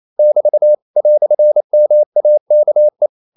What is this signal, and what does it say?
Morse code, heard as a keyed tone, about 28 wpm, 600 Hz. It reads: <BT> "MAKE
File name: CW / Practice / EnglishQuotations0228